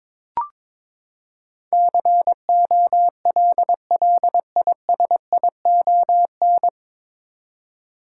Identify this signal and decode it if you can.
Morse code, heard as a keyed tone, about 21 wpm, 700 Hz. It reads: COLLISION
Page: Morse Code Ninja practice 22wpm